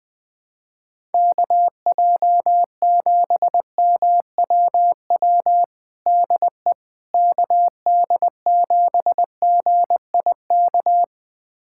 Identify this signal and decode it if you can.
Morse code, heard as a keyed tone, about 21 wpm, 700 Hz. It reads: KJ7MWW DE KD7GIK
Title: USA Callsign DE Callsign - 6 Characters - 20wpm